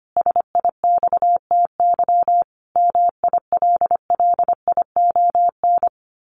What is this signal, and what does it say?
Morse code, heard as a keyed tone, about 25 wpm, 700 Hz. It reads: SIXTY MILLION
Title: A Sound Of Thunder 25wpm